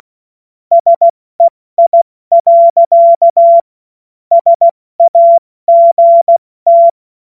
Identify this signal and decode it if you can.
Morse code, heard as a keyed tone, about 16 wpm, 680 Hz. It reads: SEI. SAGT
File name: Morsekurs / Witze / Witz 21.November 2023 70 F80